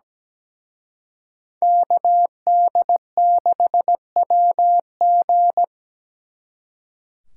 Morse code, heard as a keyed tone, about 20 wpm, 700 Hz. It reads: KD6WG
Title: Speed Racing - USA 2x2 Call Signs 17wpm